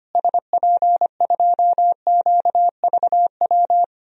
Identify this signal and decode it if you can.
Morse code, heard as a keyed tone, about 25 wpm, 700 Hz. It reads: SP2QVW
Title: DXCC-100 Call Signs - 6 Characters - Repeated 3x - 25wpm